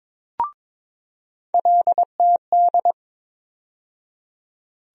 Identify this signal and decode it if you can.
Morse code, heard as a keyed tone, about 22 wpm, 700 Hz. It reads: LTD
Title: USA Call Sign Postfixes - 3 Letters 22wpm